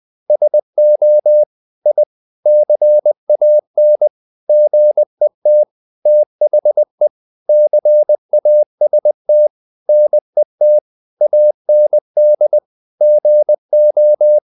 Morse code, heard as a keyed tone, about 20 wpm, 600 Hz. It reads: SO I CAN GET THE CAST NET AND GO
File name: the old man and the sea / the old man and the sea part 1 20wpm